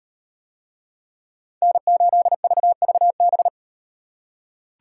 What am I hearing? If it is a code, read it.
Morse code, heard as a keyed tone, about 38 wpm, 700 Hz. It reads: N8VVB